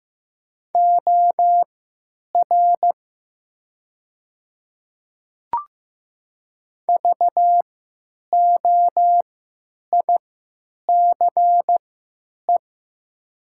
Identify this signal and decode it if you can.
Morse code, heard as a keyed tone, about 15 wpm, spaced as for 9 wpm, 700 Hz. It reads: OR VOICE
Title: 3x Character Spacing - Top 500 Words - 15wpm